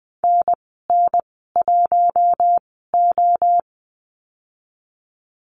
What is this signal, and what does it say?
Morse code, heard as a keyed tone, about 20 wpm, spaced as for 15 wpm, 700 Hz. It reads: NN1O
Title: 048 Lesson 9g - Number 4 - Call Signs 20wpm